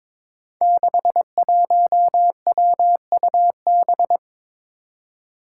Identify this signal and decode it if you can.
Morse code, heard as a keyed tone, about 22 wpm, 700 Hz. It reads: 61WUB
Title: Five Challenging Letter-Number Combinations 22wpm